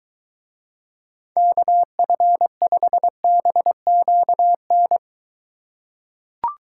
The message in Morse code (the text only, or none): KF5BQN